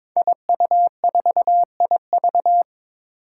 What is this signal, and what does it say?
Morse code, heard as a keyed tone, about 22 wpm, 700 Hz. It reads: IU4IV
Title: Five Challenging Letter-Number Combinations 22wpm